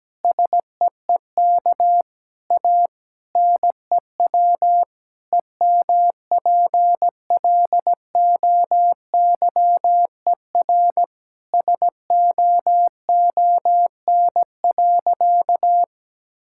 Morse code, heard as a keyed tone, about 17 wpm, 700 Hz. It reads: SEEK A NEW EMPLOYER SOON.
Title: Sentences from Top 1600 Words 17wpm